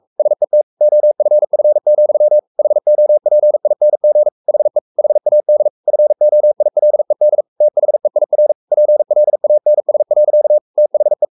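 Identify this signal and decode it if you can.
Morse code, heard as a keyed tone, about 44 wpm, 600 Hz. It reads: SET OFF, HOPING HE HAD FOILED THEIR PLANS. THE
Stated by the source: Morse code at 43 wpm sending Ken Follett's novel Never in English, from CW med Gnister.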